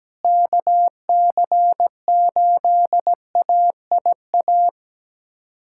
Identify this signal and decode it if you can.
Morse code, heard as a keyed tone, about 17 wpm, 700 Hz. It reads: KC8AIA
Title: SOTA QSO Practice 17wpm